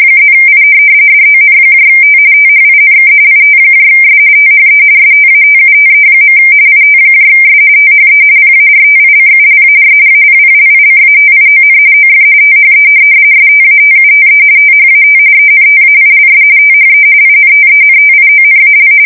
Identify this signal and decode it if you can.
RTTY, about 45.45 baud. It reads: +I DONT THINK ITS THE STATUTE. ITS THE PARTICULAR APPLICATION OF THE STATUTE THAT ENTAILS THE INTENTIONAL
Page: 170 Shift 2125/2295